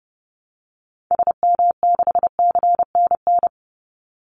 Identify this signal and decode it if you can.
Morse code, heard as a keyed tone, about 30 wpm, 700 Hz. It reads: SM6CNN
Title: Call Signs - 30wpm